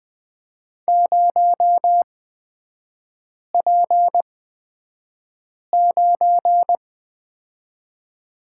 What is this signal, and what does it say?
Morse code, heard as a keyed tone, about 20 wpm, spaced as for 5 wpm, 700 Hz. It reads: 0P9
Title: 243 Lesson 39e - Letter z - Three Characters 20wpm